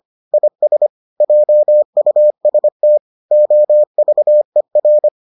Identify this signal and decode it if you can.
Morse code, heard as a keyed tone, about 25 wpm, 600 Hz. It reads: IS JUST OVER